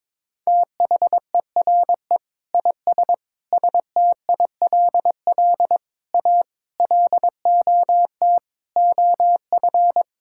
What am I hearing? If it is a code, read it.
Morse code, heard as a keyed tone, about 22 wpm, 700 Hz. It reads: THERE IS STILL A LOT OF
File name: Sets of 9 Words - 22wpm